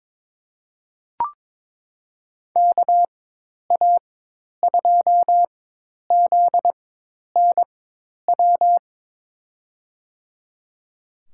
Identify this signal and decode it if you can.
Morse code, heard as a keyed tone, about 22 wpm, spaced as for 10 wpm, 700 Hz. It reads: KA2ZNW
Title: USA Call Signs - 6 Characters - 4x Character Spacing - 22wpm